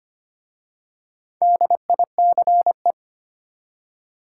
Morse code, heard as a keyed tone, about 25 wpm, 690 Hz. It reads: DICE
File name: practice / 25wpm / Easy Words with Hints - 25wpm